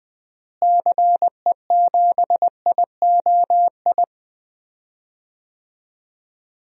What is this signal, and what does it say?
Morse code, heard as a keyed tone, about 20 wpm, 700 Hz. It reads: CE7IOI
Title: DXCC-100 Call Signs - 6 Characters - Repeated 3x - 20wpm